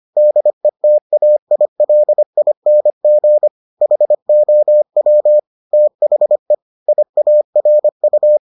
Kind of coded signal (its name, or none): Morse code